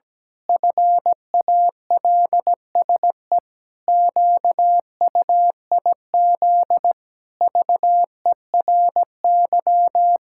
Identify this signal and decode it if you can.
Morse code, heard as a keyed tone, about 17 wpm, 700 Hz. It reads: FALSE QUIZ VERY
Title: Pangram Challenge v2 - 17wpm